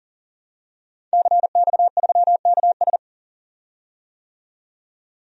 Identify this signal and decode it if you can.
Morse code, heard as a keyed tone, about 39 wpm, 700 Hz. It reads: CX3KS
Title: DXCC-100 Call Signs - 5 Characters - 40wpm